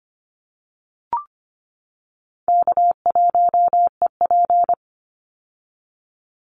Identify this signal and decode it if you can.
Morse code, heard as a keyed tone, about 25 wpm, 700 Hz. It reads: K1EP
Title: Call Signs - 25wpm